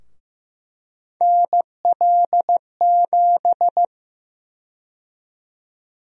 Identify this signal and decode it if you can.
Morse code, heard as a keyed tone, about 16 wpm, 700 Hz. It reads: NL7
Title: USA Call Sign Prefixes - 15wpm